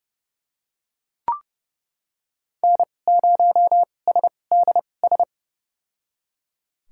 Morse code, heard as a keyed tone, about 32 wpm, spaced as for 22 wpm, 700 Hz. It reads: N0SDS
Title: USA Call Signs - 5 Characters - 2x Character Spacing - 30wpm